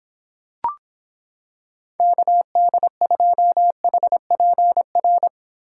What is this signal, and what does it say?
Morse code, heard as a keyed tone, about 26 wpm, 700 Hz. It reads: KD2HPR